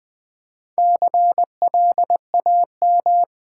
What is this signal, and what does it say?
Morse code, heard as a keyed tone, about 20 wpm, 690 Hz. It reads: CLAM
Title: Difficult Words with Hints - 20wpm